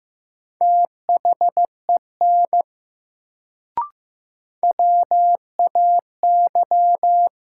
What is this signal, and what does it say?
Morse code, heard as a keyed tone, about 15 wpm, 700 Hz. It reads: THEN WAY
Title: Sets of 2 Words - New Word Drill - 15wpm